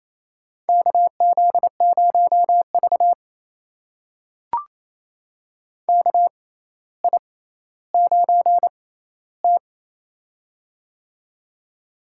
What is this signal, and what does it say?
Morse code, heard as a keyed tone, about 28 wpm, spaced as for 14 wpm, 700 Hz. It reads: KZ0V KI9T
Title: USA Call Signs - 4 Characters - 6x Character Spacing - 28wpm